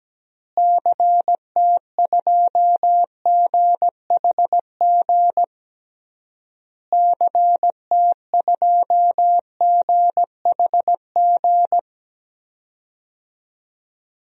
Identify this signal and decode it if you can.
Morse code, heard as a keyed tone, about 17 wpm, 700 Hz. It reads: CT2GHG CT2GHG
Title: DXCC-100 Call Signs - 6 Characters - Repeated 3x - 17wpm